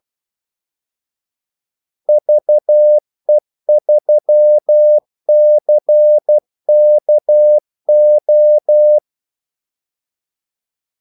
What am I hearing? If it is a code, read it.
Morse code, heard as a keyed tone, about 12 wpm, 600 Hz. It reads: VE3CKO
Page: 2 ekstra ordmellemrum mellem kaldesignalerne